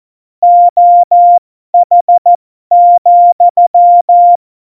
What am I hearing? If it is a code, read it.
Morse code, heard as a keyed tone, about 14 wpm, spaced as for 12 wpm, 700 Hz. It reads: OH,
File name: Morsekurs / Witze